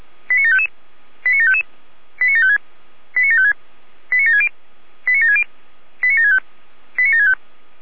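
Five-tone selective call (ZVEI), heard as quick runs of five tones, tones of about 70 ms. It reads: 89755 89755 89756 89756 89760 89760 89765 89765
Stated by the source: ZVEI five-tone call, code 89705, received by SDR.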